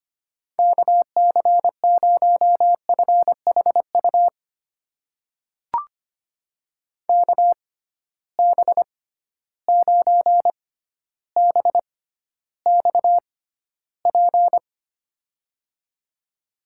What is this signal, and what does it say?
Morse code, heard as a keyed tone, about 25 wpm, spaced as for 13 wpm, 700 Hz. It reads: KC0FHU KB9BXP
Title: USA Call Signs - 6 Characters - 6x Character Spacing - 25wpm